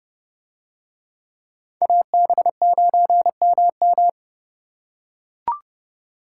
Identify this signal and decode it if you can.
Morse code, heard as a keyed tone, about 30 wpm, 700 Hz. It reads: AB9MM